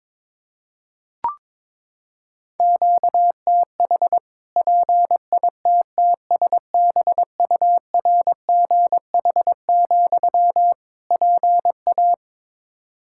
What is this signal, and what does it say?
Morse code, heard as a keyed tone, about 21 wpm, 700 Hz. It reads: QTH PITTSBURGH, PA
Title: Practice QSO - Exchange 1 - Beginning - 22wpm